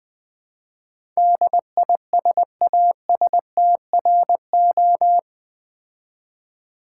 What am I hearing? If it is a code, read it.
Morse code, heard as a keyed tone, about 20 wpm, 700 Hz. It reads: DISASTRO
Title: Build A Word - 4 Letters At A Time - 10-Letter Words - 20wpm